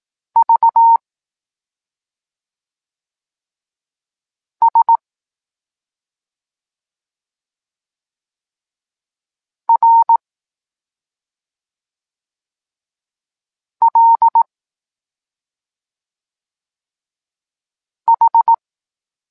Morse code, heard as a keyed tone, about 18 wpm, 920 Hz. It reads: V S R L H
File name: Tekst-26 1